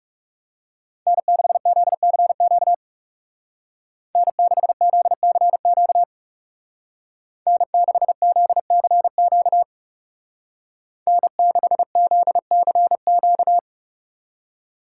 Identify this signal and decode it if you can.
Morse code, heard as a keyed tone, about 35 wpm, 700 Hz. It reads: N6ZCQ N6ZCQ N6ZCQ N6ZCQ
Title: Speed Racing - USA Call Signs 30wpm